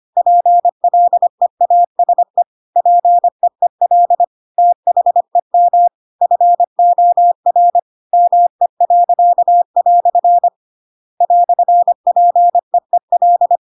Morse code, heard as a keyed tone, about 25 wpm, 700 Hz. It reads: PLEASE PEEL THEM FOR ME." "PEEL
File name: Pinocchio / Pinocchio ch7 25 25 ik1hge